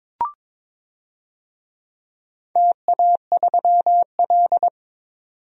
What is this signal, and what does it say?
Morse code, heard as a keyed tone, about 22 wpm, 700 Hz. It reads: TA3L
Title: DXCC-100 Call Signs - 4 Characters - 22wpm